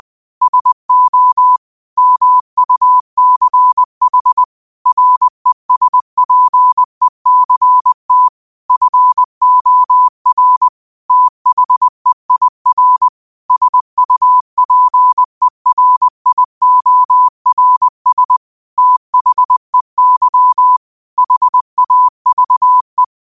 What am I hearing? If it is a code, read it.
Morse code, heard as a keyed tone, about 20 wpm, 1000 Hz. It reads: SO MUCH RESPECT FOR THEIR SUPERIORS THEY HAVE
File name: qotdfiles / 20wpm